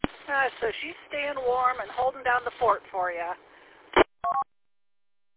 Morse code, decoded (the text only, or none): none